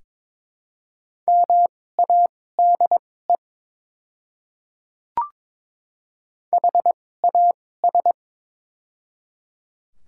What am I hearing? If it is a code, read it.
Morse code, heard as a keyed tone, about 22 wpm, spaced as for 16 wpm, 700 Hz. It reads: MADE HAS
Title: 2x Character Spacing - Top 300 Words - 22wpm